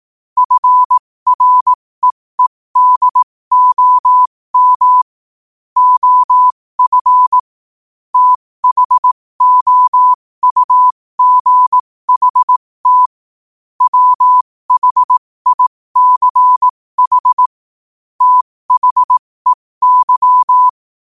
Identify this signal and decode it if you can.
Morse code, heard as a keyed tone, about 18 wpm, spaced as for 15 wpm, 1000 Hz. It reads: FREEDOM OF THOUGHT WHICH THEY
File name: qotdfiles / 15wpm